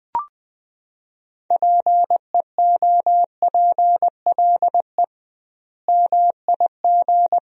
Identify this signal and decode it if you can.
Morse code, heard as a keyed tone, about 21 wpm, 700 Hz. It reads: PEOPLE MIG
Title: 2x Word-Spacing - Sentences from Top 500 Words - 20wpm